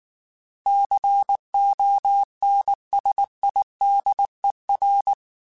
Morse code, heard as a keyed tone, about 19 wpm, 800 Hz. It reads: CONSIDER
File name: Technical Subjects / Technical Subjects-mp3s / AND Gate @18